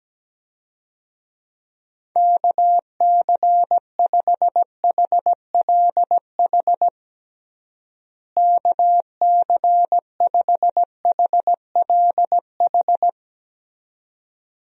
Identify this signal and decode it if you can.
Morse code, heard as a keyed tone, about 17 wpm, 700 Hz. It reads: KC5HLH KC5HLH
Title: USA Call Signs - 6 Characters - Repeated 2x - 17wpm